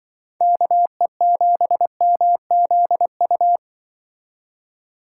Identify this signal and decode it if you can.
Morse code, heard as a keyed tone, about 24 wpm, 700 Hz. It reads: KE7MZU
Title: Speed Racing - USA 2x3 Call Signs 20wpm